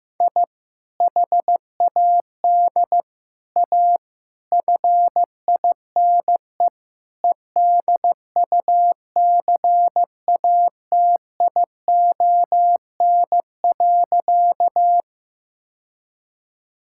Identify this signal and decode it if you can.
Morse code, heard as a keyed tone, about 15 wpm, 700 Hz. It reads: I HAD A FINE EDUCATION.
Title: Sentences from Top 1000 Words - Review Part 6 - 15wpm